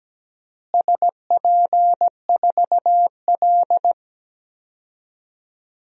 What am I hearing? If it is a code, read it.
Morse code, heard as a keyed tone, about 17 wpm, 700 Hz. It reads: SP4L
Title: DXCC-100 Call Signs - 4 Characters - Repeated 3x - 17wpm